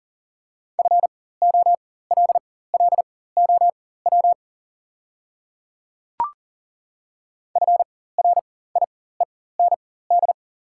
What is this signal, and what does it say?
Morse code, heard as a keyed tone, about 40 wpm, spaced as for 19 wpm, 690 Hz. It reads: FOLLOW FRIEND
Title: 4x Character Spacing - Top 300 Words - 40wpm